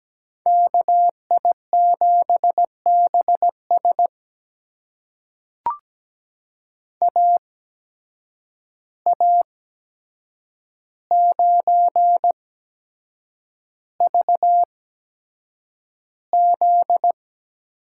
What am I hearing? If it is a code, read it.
Morse code, heard as a keyed tone, about 17 wpm, spaced as for 7 wpm, 700 Hz. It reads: KI7BS AA9VZ
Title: USA Call Signs - 5 Characters - 8x Character Spacing - 17wpm